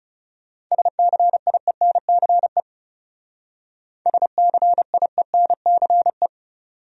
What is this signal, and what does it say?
Morse code, heard as a keyed tone, about 34 wpm, 700 Hz. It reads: SCIENCE SCIENCE
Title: Speed-Racing Top 500 Words 30wpm